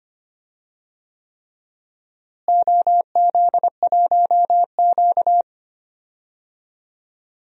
Morse code, heard as a keyed tone, about 25 wpm, 700 Hz. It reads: OZ1Q
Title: DXCC-100 Call Signs - 4 Characters - Repeated 3x - 25wpm